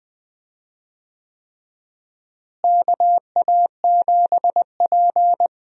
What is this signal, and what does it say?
Morse code, heard as a keyed tone, about 20 wpm, 700 Hz. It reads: KA7P
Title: USA Call Signs - 4 Characters - Repeated 3x - 20wpm